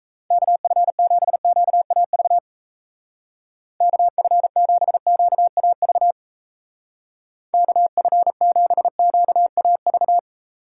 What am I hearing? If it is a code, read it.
Morse code, heard as a keyed tone, about 34 wpm, 700 Hz. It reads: KF7QAV KF7QAV KF7QAV
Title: Speed Racing - USA Call Signs 28wpm